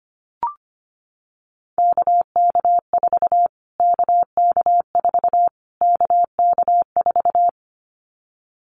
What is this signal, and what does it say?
Morse code, heard as a keyed tone, about 25 wpm, 700 Hz. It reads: KK4 KK4 KK4
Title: Callsign Prefixes for Top 25 DXCC - Repeated 3x - 25wpm